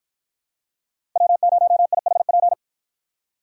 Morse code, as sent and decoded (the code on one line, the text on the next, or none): .-- ----- .. .... .--.
W0IHP